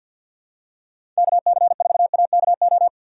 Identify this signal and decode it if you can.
Morse code, heard as a keyed tone, about 52 wpm, 700 Hz. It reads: KC4AKO
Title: USA Call Signs - 6 Characters - 4x Character Spacing - 50wpm